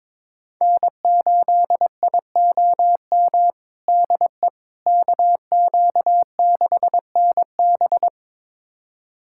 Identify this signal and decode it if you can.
Morse code, heard as a keyed tone, about 22 wpm, 700 Hz. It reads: N8IOM DE KQ6NB
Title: USA Callsign DE Callsign - 5 Characters - 22wpm